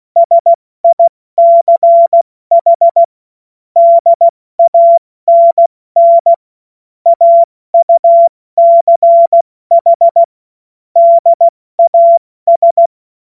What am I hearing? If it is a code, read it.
Morse code, heard as a keyed tone, about 16 wpm, 680 Hz. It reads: SICH DANN AUCH DAS
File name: Witz-09-September-2025-70-F80